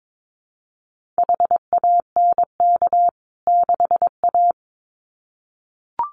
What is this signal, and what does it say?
Morse code, heard as a keyed tone, about 22 wpm, 700 Hz. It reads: HANK 6A
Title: CWT Exchange 22wpm